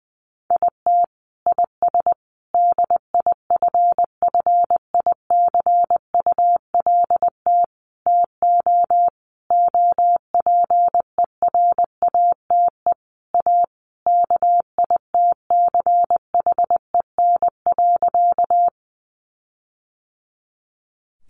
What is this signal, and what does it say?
Morse code, heard as a keyed tone, about 20 wpm, 700 Hz. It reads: IT IS DIFFICULT TO OPERATE A KITCHEN.
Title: Sentences from Top 1500 Words 20wpm